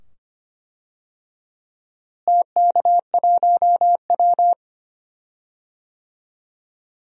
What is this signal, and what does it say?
Morse code, heard as a keyed tone, about 25 wpm, 700 Hz. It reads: TK1W
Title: DXCC-100 Call Signs - 4 Characters - Repeated 2x - 25wpm